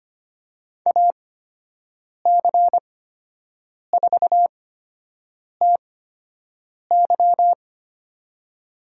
Morse code, heard as a keyed tone, about 25 wpm, spaced as for 7 wpm, 700 Hz. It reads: AC4TY
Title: USA Call Signs - 8x Character Spacing - 25wpm